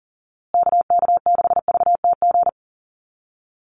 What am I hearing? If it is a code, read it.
Morse code, heard as a keyed tone, about 40 wpm, 700 Hz. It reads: KK6VTG
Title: USA Call Signs Repeated 2x 40wpm